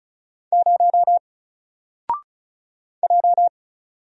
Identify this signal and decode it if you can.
Morse code, heard as a keyed tone, about 33 wpm, 700 Hz. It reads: 0 J
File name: Single-Letter-Number-35wpm